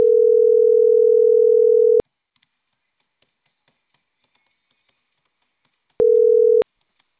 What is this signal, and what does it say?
Ringback tone (North America), 440 + 480 Hz.